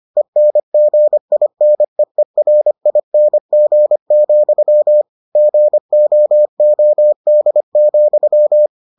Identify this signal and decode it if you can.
Morse code, heard as a keyed tone, about 25 wpm, 600 Hz. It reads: ENGINEERING, GOOD,